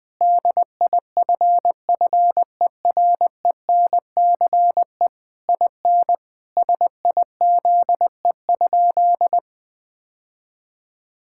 Morse code, heard as a keyed tone, about 21 wpm, 700 Hz. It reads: DIFFERENCE IN SIZE?
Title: Sentences from Top 1000 Words - Review Part 5 - 20wpm